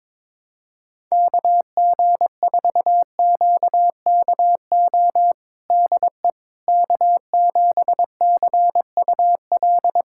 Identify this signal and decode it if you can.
Morse code, heard as a keyed tone, about 21 wpm, 700 Hz. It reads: KG4QKO DE K7CUL
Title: USA Callsign DE Callsign 22wpm